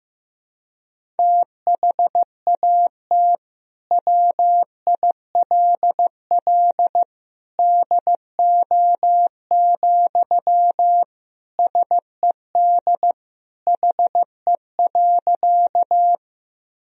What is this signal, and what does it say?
Morse code, heard as a keyed tone, about 16 wpm, 700 Hz. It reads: THAT WILL DO, SED HE.
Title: Sentences from Top 100 Words and Abbreviations 15wpm